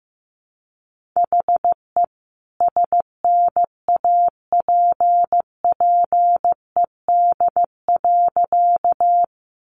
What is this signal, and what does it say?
Morse code, heard as a keyed tone, about 15 wpm, 700 Hz. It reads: HE SNAPPED.
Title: A Sound Of Thunder 15wpm